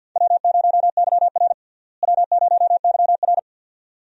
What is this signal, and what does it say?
Morse code, heard as a keyed tone, about 51 wpm, 700 Hz. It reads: W0YR W0YR
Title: USA Call Signs - 4 Characters - Repeated 2x - 50wpm